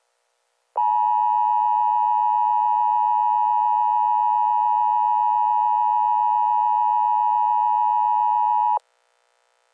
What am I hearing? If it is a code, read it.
Emergency Alert System (SAME) tones (the 853 and 960 Hz attention tone). It is heard from 0.8 to 8.8 s.